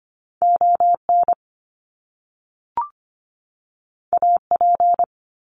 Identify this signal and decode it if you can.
Morse code, heard as a keyed tone, about 25 wpm, 700 Hz. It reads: ON AP
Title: Top Two Letter Combinations in Words 25wpm